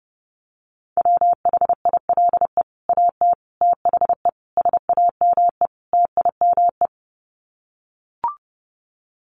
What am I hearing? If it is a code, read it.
Morse code, heard as a keyed tone, about 30 wpm, 700 Hz. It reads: WHILE AT THE SAME TIME
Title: Sets of 5 Words 30wpm